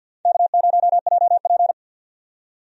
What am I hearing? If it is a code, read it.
Morse code, heard as a keyed tone, about 49 wpm, 700 Hz. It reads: K0JP
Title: USA Call Signs - 4 Characters - 50wpm